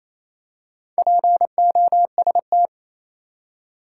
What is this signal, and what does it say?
Morse code, heard as a keyed tone, about 28 wpm, 700 Hz. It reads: POST
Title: Four Letter Words - 28wpm